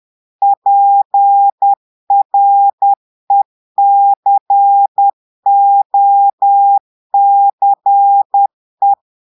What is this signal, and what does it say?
Morse code, heard as a keyed tone, about 10 wpm, 800 Hz. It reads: PRECOCE
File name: Portugais - texte en clair 1 - vitesse - 10 mots minute